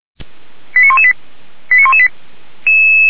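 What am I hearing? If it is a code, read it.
Five-tone selective call (ZVEI), heard as quick runs of five tones, tones of about 70 ms. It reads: 89118 89118
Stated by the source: ZVEI five-tone call 89321 received via SDR.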